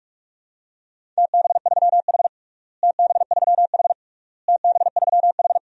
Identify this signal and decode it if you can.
Morse code, heard as a keyed tone, about 44 wpm, 700 Hz. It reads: TB3H TB3H TB3H
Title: DXCC-100 Call Signs - 4 Characters - Repeated 3x - 45wpm